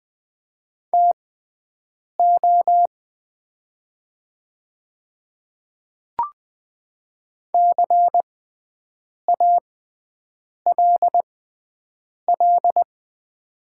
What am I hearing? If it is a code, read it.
Morse code, heard as a keyed tone, about 20 wpm, spaced as for 7 wpm, 690 Hz. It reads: TO CALL
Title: 6x Character Spacing - Top 100 Words - 20wpm